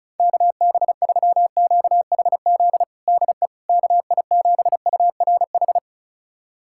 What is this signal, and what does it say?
Morse code, heard as a keyed tone, about 36 wpm, 700 Hz. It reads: KB3QHZ DE KI7URH
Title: USA Callsign DE Callsign - 6 Characters - 35wpm